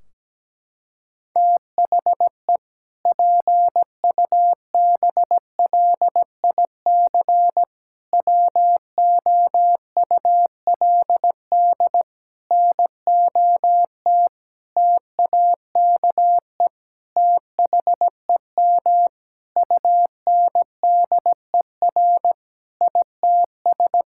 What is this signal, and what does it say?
Morse code, heard as a keyed tone, about 17 wpm, 700 Hz. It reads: THE PUBLIC WOULD NOT TAKE THEM UNDER ITS
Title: Sentences from Top 1600 Words 17wpm